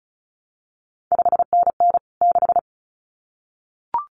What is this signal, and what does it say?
Morse code, heard as a keyed tone, about 36 wpm, 700 Hz. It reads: 5NN 6
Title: CQ WW DX Contest Exchange Encore - 35wpm